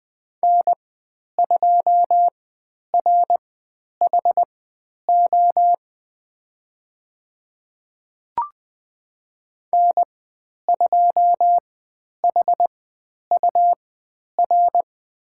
Morse code, heard as a keyed tone, about 20 wpm, spaced as for 10 wpm, 700 Hz. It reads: N2RHO N2HUR
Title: 102 Lesson 17g - Letter u - Call Signs 20wpm